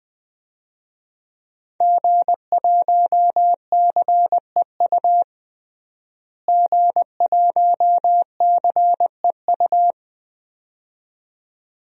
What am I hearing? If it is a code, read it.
Morse code, heard as a keyed tone, about 20 wpm, 700 Hz. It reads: G1CEU G1CEU
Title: DXCC-100 Call Signs - 5 Characters - Repeated 2x - 20wpm